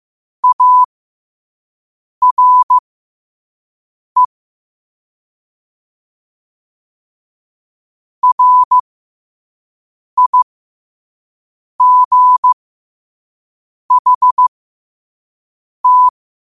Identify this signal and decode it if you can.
Morse code, heard as a keyed tone, about 15 wpm, spaced as for 5 wpm, 1000 Hz. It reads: ARE RIGHT
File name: qotdfiles / 5wpm